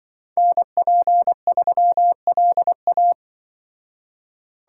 Morse code, heard as a keyed tone, about 25 wpm, 700 Hz. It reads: NP3LA
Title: Speed Racing - USA Call Signs 20wpm